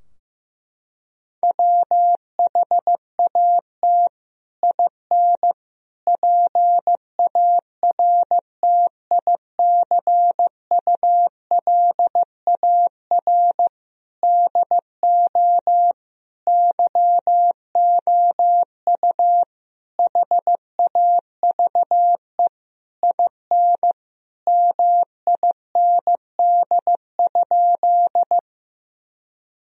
Morse code, heard as a keyed tone, about 15 wpm, 700 Hz. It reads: WHAT IN PARTICULAR DO YOU HAVE IN MIND?
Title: Sentences from Top 1000 Words - Review Part 4 - 15wpm